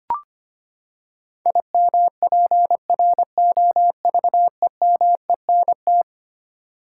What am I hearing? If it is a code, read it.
Morse code, heard as a keyed tone, about 25 wpm, 700 Hz. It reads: IMPROVEMENT
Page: morse-code-ninja-practice-25wpm